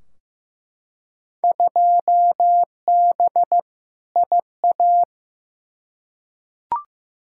Morse code, heard as a keyed tone, about 16 wpm, 700 Hz. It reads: 2B IA
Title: Field Day Exchanges Encore - 15wpm